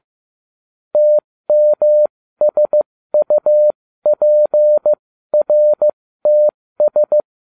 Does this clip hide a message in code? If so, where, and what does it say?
Yes: Morse code from 1.0 to 7.2 s, heard as a keyed tone, about 15 wpm, spaced as for 13 wpm, 600 Hz. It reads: TMSUPRTS